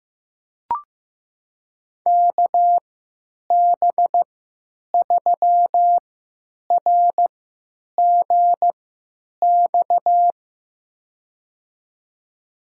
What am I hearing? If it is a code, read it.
Morse code, heard as a keyed tone, about 15 wpm, spaced as for 9 wpm, 700 Hz. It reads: KB3RGX
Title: USA Call Signs - 3x Character Spacing - 15wpm